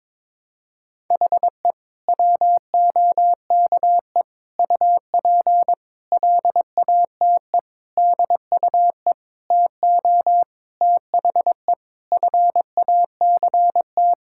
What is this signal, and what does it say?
Morse code, heard as a keyed tone, about 22 wpm, 700 Hz. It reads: HE WOKE UP LATE DUE TO THE FACT
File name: practice / 22wpm / Sentences with 4-Word Phrases - 22wpm